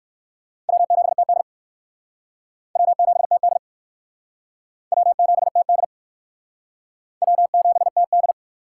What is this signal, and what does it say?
Morse code, heard as a keyed tone, about 56 wpm, 700 Hz. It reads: W7TD W7TD W7TD W7TD
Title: Speed Racing - USA 1x2 Call Signs 45wpm